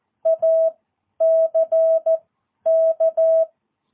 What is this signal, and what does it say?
Morse code, heard as a keyed tone, about 14 wpm, spaced as for 10 wpm, 650 Hz. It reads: ACK
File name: jh8cbh / 2022cw